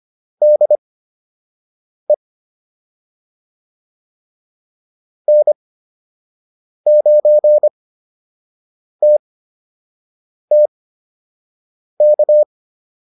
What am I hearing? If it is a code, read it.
Morse code, heard as a keyed tone, about 25 wpm, spaced as for 6 wpm, 600 Hz. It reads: DE N9TTK